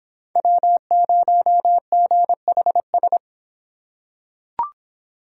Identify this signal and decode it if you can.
Morse code, heard as a keyed tone, about 24 wpm, 700 Hz. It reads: W0GHS